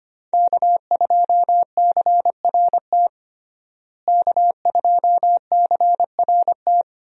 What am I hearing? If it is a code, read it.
Morse code, heard as a keyed tone, about 25 wpm, 700 Hz. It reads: K2CRT K2CRT
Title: USA Call Signs - 5 Characters - Repeated 3x - 25wpm